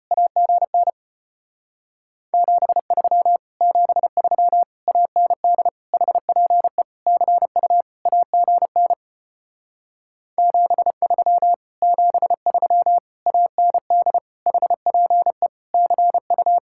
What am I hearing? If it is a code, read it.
Morse code, heard as a keyed tone, about 33 wpm, 700 Hz. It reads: AGN 73 73 AND HPE CU AGN 73 73 AND HPE CU